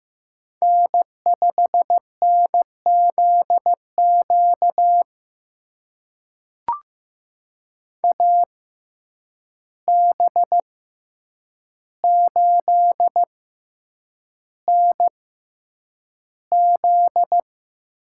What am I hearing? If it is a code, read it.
Morse code, heard as a keyed tone, about 15 wpm, spaced as for 8 wpm, 700 Hz. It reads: N5NZQ AB8NZ
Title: USA Call Signs - 5 Characters - 6x Character Spacing - 15wpm